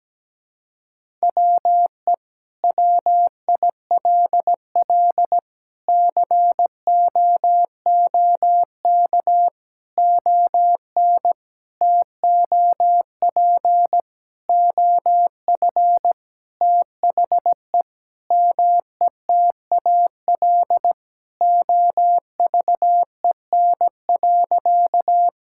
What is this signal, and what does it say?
Morse code, heard as a keyed tone, about 17 wpm, 700 Hz. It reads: WE WILL COOK ON TOP OF THE METAL OVEN.
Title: Sentences from Top 1000 Words - Review Part 6 - 17wpm